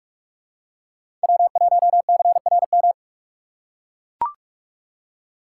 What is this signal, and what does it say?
Morse code, heard as a keyed tone, about 41 wpm, 700 Hz. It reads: W1CRM
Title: USA Call Signs - 5 Characters - 2x Character Spacing - 45wpm